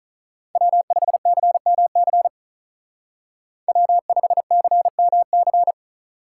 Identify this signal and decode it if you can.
Morse code, heard as a keyed tone, about 46 wpm, 700 Hz. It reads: W5CMC W5CMC